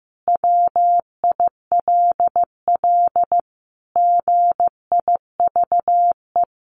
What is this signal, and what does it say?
Morse code, heard as a keyed tone, about 15 wpm, 700 Hz. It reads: WILL GIVE
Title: Sentences from Top 1100 Words 15wpm